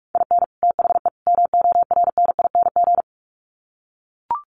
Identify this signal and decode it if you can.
Morse code, heard as a keyed tone, about 44 wpm, 700 Hz. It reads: IN THE MORNING
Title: Sets of 3 Words 45wpm